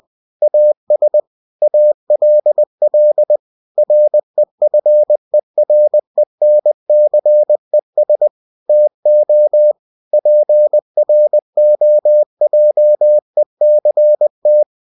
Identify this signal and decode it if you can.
Morse code, heard as a keyed tone, about 20 wpm, 600 Hz. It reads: AS ALL REFERENCES TO PROJECT
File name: cw-books / winnie-the-pooh / winnie-20wpm-page-0114